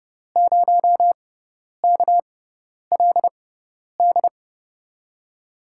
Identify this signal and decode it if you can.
Morse code, heard as a keyed tone, about 32 wpm, spaced as for 10 wpm, 700 Hz. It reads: 0KLD
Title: USA Call Signs - 6 Characters - 6x Character Spacing - 30wpm